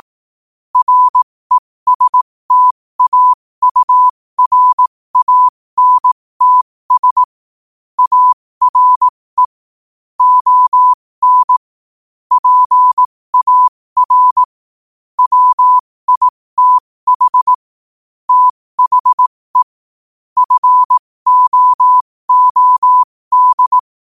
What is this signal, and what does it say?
Morse code, heard as a keyed tone, about 18 wpm, spaced as for 15 wpm, 1000 Hz. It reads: RESTAURANTS ARE ON PAR WITH THE FOOD
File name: qotdfiles / 15wpm